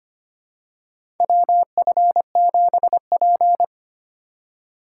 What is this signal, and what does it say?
Morse code, heard as a keyed tone, about 25 wpm, 700 Hz. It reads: WF7P
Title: USA Call Signs - 4 Characters - 2x Character Spacing - 25wpm